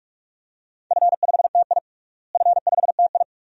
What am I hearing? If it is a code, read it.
Morse code, heard as a keyed tone, about 42 wpm, 700 Hz. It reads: F5TI F5TI
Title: DXCC-100 Call Signs - 4 Characters - Repeated 2x - 45wpm